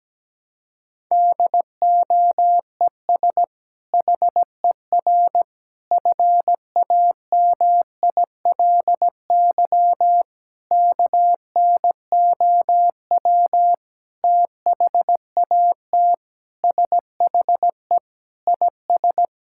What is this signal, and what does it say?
Morse code, heard as a keyed tone, about 17 wpm, 700 Hz. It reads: DOES HER FAMILY KNOW THAT SHE IS
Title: Sentences from Top 1000 Words - Review Part 6 - 17wpm